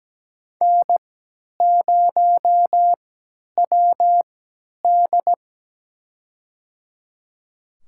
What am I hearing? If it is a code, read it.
Morse code, heard as a keyed tone, about 17 wpm, spaced as for 10 wpm, 700 Hz. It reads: N0WD
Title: USA Call Signs - 4 Characters - 3x Character Spacing - 17wpm